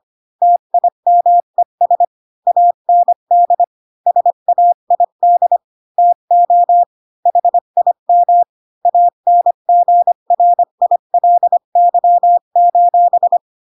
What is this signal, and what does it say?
Morse code, heard as a keyed tone, about 25 wpm, 700 Hz. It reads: TIMES AND SAID TO HIM ANGRILY:
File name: Pinocchio / Pinocchio ch3 20 25 ik1hge